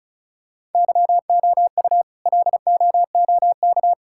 Morse code, heard as a keyed tone, about 34 wpm, 700 Hz. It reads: YOU LOOK
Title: Sentences from Top 1000 Words - Review Part 5 - 35wpm